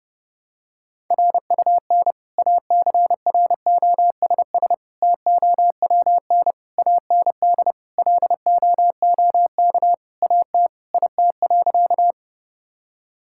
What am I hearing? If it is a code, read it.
Morse code, heard as a keyed tone, about 29 wpm, 700 Hz. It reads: RUN ACROSS TOWN AND LOOK AT IT.
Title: Sentences from Top 1000 Words - Review Part 5 - 30wpm